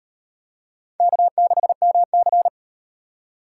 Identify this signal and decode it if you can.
Morse code, heard as a keyed tone, about 38 wpm, 700 Hz. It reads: K6MC